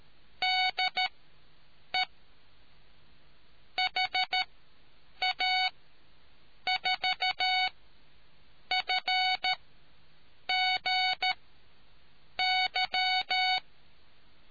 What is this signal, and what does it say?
Morse code, heard as a keyed tone, about 13 wpm, spaced as for 7 wpm, 740 Hz. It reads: DE HA4FGY